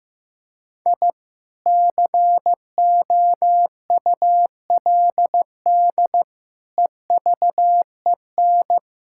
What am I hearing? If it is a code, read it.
Morse code, heard as a keyed tone, about 15 wpm, 700 Hz. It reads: I COULD EVEN
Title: Sentences from Top 1000 Words - Review Part 5 - 15wpm